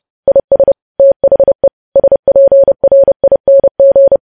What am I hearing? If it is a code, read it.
Morse code, heard as a keyed tone, about 30 wpm, 550 Hz. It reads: IS THE SPRING
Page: news headlines in morse code at 30 wpm